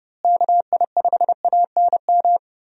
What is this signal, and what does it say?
Morse code, heard as a keyed tone, about 30 wpm, 700 Hz. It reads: KI5ANM
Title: USA Call Signs - 6 Characters - Repeated 3x - 30wpm